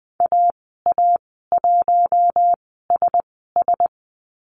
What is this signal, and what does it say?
Morse code, heard as a keyed tone, about 20 wpm, spaced as for 14 wpm, 700 Hz. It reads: AA1SS
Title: 042 Lesson 8g - Number 1 - Call Signs 20wpm